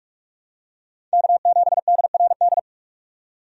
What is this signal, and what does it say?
Morse code, heard as a keyed tone, about 46 wpm, 700 Hz. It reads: K7DRD
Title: USA Call Signs - 5 Characters - 45wpm